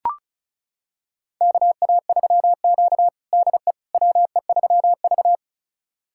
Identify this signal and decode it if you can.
Morse code, heard as a keyed tone, about 36 wpm, 700 Hz. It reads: KA3Q DE WE3V
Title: USA Callsign DE Callsign - 4 Characters - 35wpm